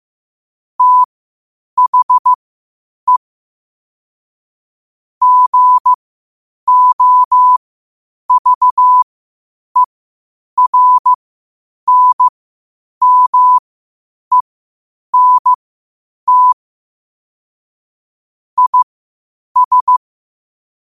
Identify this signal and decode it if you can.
Morse code, heard as a keyed tone, about 15 wpm, spaced as for 8 wpm, 1000 Hz. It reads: THE GOVERNMENT IS
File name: qotdfiles / 8wpm